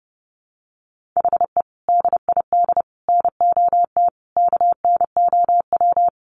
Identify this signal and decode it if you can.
Morse code, heard as a keyed tone, about 30 wpm, 700 Hz. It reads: HE DID NOT KNOW
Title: Sentences from Top 1500 Words 30wpm